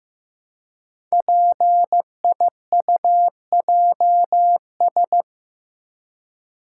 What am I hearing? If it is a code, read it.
Morse code, heard as a keyed tone, about 15 wpm, 700 Hz. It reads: PIUJS
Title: Code Groups - 5 Letters - 15wpm